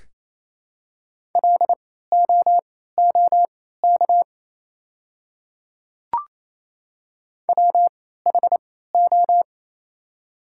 Morse code, heard as a keyed tone, about 28 wpm, spaced as for 16 wpm, 700 Hz. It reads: LOOK WHO
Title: 3x Character Spacing - Top 100 Words - 28wpm